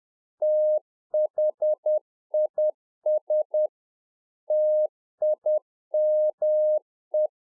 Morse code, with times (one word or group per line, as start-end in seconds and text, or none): 0.4-3.7 THIS
4.5-7.3 TIME